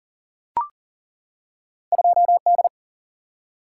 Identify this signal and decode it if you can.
Morse code, heard as a keyed tone, about 42 wpm, 700 Hz. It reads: 2D
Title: Single-Letter Single-Number 40wpm